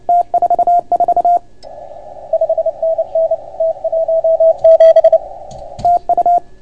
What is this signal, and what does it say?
Morse code, heard as a keyed tone, about 29 wpm, 690 Hz, with radio noise. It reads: T44 5NN T17 TU